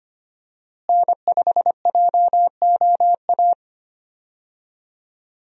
Morse code, heard as a keyed tone, about 25 wpm, 700 Hz. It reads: N5JOA
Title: USA Call Signs - 5 Characters - Repeated 3x - 25wpm